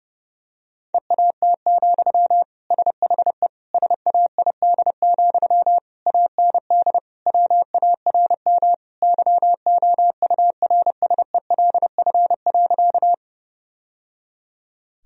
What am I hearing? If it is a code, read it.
Morse code, heard as a keyed tone, about 30 wpm, 700 Hz. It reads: EAT, SHE SAID, AND WARM YOURSELF.
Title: Sentences from Top 1000 Words - Review Part 2 - 30wpm